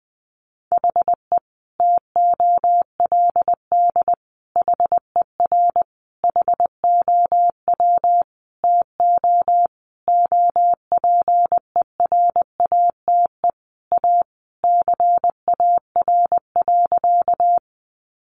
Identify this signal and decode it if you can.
Morse code, heard as a keyed tone, about 20 wpm, 700 Hz. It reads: HE TOLD HER HOW TO OPERATE A CAR.
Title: Sentences from Top 1500 Words 20wpm